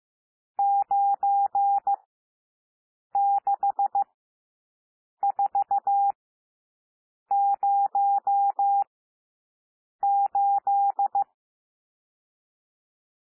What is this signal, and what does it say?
Morse code, heard as a keyed tone, about 15 wpm, spaced as for 6 wpm, 800 Hz. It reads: 96408
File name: SOUND NUMBERS 15 wpm